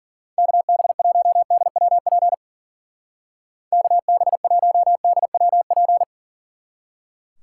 Morse code, heard as a keyed tone, about 50 wpm, 700 Hz. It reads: KB1DWP KB1DWP
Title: Speed Racing - USA Call Signs 40wpm